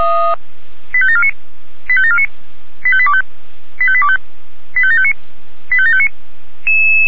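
Five-tone selective call (ZVEI), heard as quick runs of five tones, tones of about 70 ms. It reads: 85749 85749 85825 85825 85759 85759
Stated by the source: ZVEI five-tone selective call received by SDR, code 85827.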